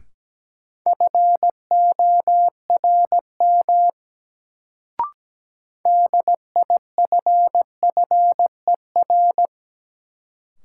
Morse code, heard as a keyed tone, about 17 wpm, 700 Hz. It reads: FORM DIFFER
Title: Sentences from Top 1000 Words - Review Part 1 - New Word Drill - 17wpm